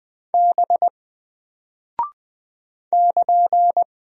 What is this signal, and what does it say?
Morse code, heard as a keyed tone, about 20 wpm, 700 Hz. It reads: B <KN>
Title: Single Character (Letter:Number:Punctuation:Prosign) Alt 20wpm